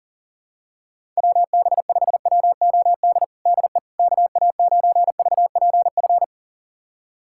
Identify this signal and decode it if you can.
Morse code, heard as a keyed tone, about 37 wpm, 690 Hz. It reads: WB5WOD DE KA9VPF
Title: USA Callsign DE Callsign - 6 Characters - 40wpm